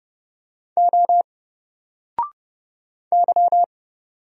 Morse code, heard as a keyed tone, about 29 wpm, 700 Hz. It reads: O Y
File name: Single-Letter-Number-30wpm